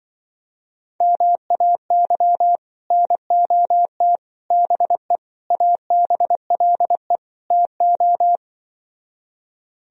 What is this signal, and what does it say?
Morse code, heard as a keyed tone, about 24 wpm, 700 Hz. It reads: MAY NOT BE ABLE TO
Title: Speed-Racing - Sets of 5 Words 20wpm